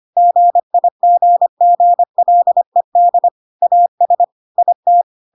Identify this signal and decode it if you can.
Morse code, heard as a keyed tone, about 25 wpm, 700 Hz. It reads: GIGGLED AS IT
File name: Pinocchio / Pinocchio ch1 25 25 ik1hge